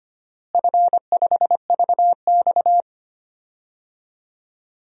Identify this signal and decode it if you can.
Morse code, heard as a keyed tone, about 25 wpm, 700 Hz. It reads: F5VX
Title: DXCC-100 Call Signs - 4 Characters - Repeated 3x - 25wpm